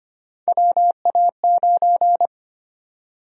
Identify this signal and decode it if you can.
Morse code, heard as a keyed tone, about 25 wpm, 700 Hz. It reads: WA9
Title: USA Call Sign Prefixes - Repeated 2x - 25wpm